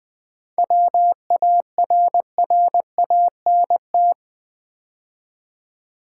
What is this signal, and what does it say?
Morse code, heard as a keyed tone, about 20 wpm, 700 Hz. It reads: WARRANT
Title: 7 Letter Words - First 3 Letters Given - 20wpm